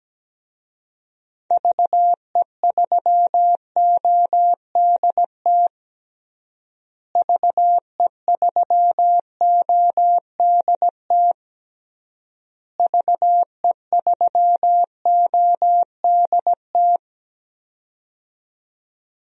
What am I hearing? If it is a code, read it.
Morse code, heard as a keyed tone, about 17 wpm, 700 Hz. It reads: VE3ODT VE3ODT VE3ODT
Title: DXCC-100 Call Signs - 6 Characters - Repeated 3x - 17wpm